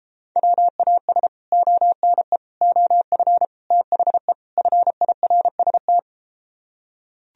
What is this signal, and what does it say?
Morse code, heard as a keyed tone, about 33 wpm, 700 Hz. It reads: WAS ONE OF THE FIRST